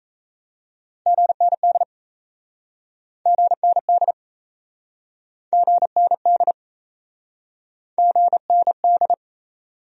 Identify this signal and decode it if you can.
Morse code, heard as a keyed tone, about 37 wpm, 700 Hz. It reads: GND GND GND GND